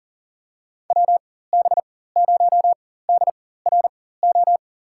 Morse code, heard as a keyed tone, about 40 wpm, spaced as for 19 wpm, 700 Hz. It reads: WB0DRO
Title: USA Call Signs - 6 Characters - 4x Character Spacing - 40wpm